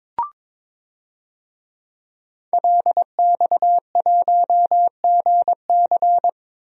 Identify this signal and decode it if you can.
Morse code, heard as a keyed tone, about 22 wpm, 700 Hz. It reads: LX1GC
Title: DXCC-100 Call Signs - 5 Characters - 22wpm